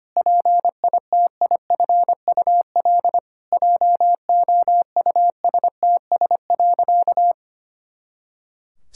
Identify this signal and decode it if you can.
Morse code, heard as a keyed tone, about 25 wpm, 700 Hz. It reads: PITIFUL JOUSTS.
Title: Pangram Challenge v2 - 25wpm